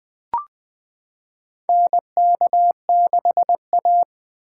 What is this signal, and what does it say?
Morse code, heard as a keyed tone, about 20 wpm, 700 Hz. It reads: NK6A
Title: 2023 SOTA Chaser Call Signs - 20wpm